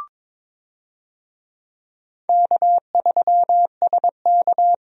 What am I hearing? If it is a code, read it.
Morse code, heard as a keyed tone, about 21 wpm, 700 Hz. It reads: K3SK
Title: USA Call Signs - 4 Characters - 22wpm